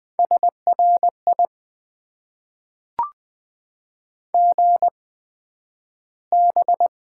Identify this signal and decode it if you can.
Morse code, heard as a keyed tone, about 20 wpm, spaced as for 11 wpm, 700 Hz. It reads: SRI GB
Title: Top 90 QSO Elements - 8x Character Spacing - 20wpm